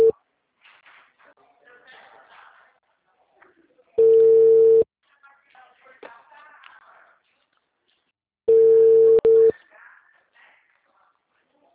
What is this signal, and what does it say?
Ringback tone (China, 450 Hz), with speech.